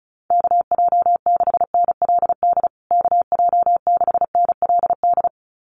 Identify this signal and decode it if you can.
Morse code, heard as a keyed tone, about 34 wpm, 700 Hz. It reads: KJ6NLD KJ6NLD
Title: USA Call Signs Repeated 2x 35wpm